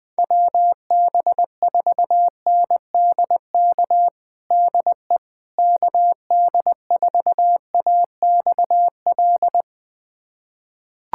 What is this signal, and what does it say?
Morse code, heard as a keyed tone, about 21 wpm, 700 Hz. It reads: WB4NDK DE KD4AXL
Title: USA Callsign DE Callsign - 6 Characters - 20wpm